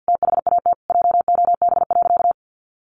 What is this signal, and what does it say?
Morse code, heard as a keyed tone, about 50 wpm, 700 Hz. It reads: THAT JOB.
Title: Sentences from Top 1100 Words 50wpm